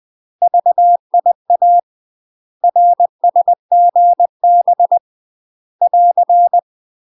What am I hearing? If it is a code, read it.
Morse code, heard as a keyed tone, about 20 wpm, 700 Hz. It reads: VIA RSGB <AR>
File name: UK MASTER QSOs TRAINING / 20WPM / UK MASTER QSOs 0012